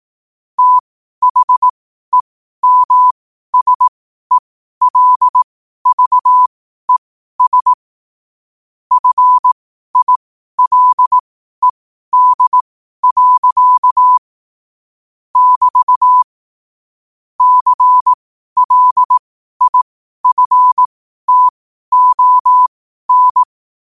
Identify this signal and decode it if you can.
Morse code, heard as a keyed tone, about 18 wpm, spaced as for 12 wpm, 1000 Hz. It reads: THEMSELVES FILED. <BT> CLIFTON
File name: qotdfiles / 12wpm